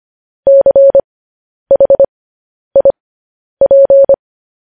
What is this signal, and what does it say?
Morse code, heard as a keyed tone, about 25 wpm, spaced as for 10 wpm, 550 Hz. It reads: CHIP
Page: news-headlines-in-morse-code-at-10-wpm